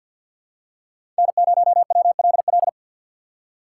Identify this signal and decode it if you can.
Morse code, heard as a keyed tone, about 53 wpm, 700 Hz. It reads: N0WLL
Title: USA Call Signs - 5 Characters - 3x Character Spacing - 50wpm